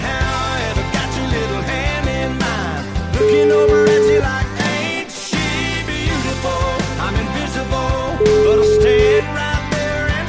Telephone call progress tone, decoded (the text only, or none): ringback tone